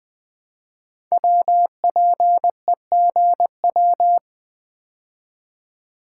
Morse code, heard as a keyed tone, about 20 wpm, 700 Hz. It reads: WPEGW
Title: Code Groups - 5 Letter:Numbers - 20wpm